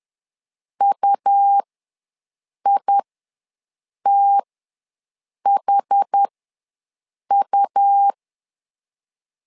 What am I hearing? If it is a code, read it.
Morse code, heard as a keyed tone, about 10 wpm, spaced as for 6 wpm, 800 Hz. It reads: UITHU
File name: LEZIONI / 2 UHDN / 2 UH ws 10wpm 2029